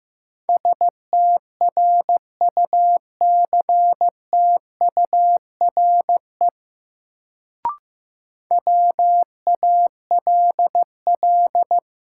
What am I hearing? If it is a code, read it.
Morse code, heard as a keyed tone, about 15 wpm, 700 Hz. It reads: STRUCTURE WALL
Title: Sentences from Top 1000 Words - Review Part 6 - New Word Drill - 15wpm